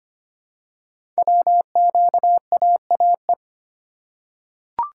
Morse code, heard as a keyed tone, about 25 wpm, 700 Hz. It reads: WQAAE
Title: Code Groups - 5 Letter:Numbers - 25wpm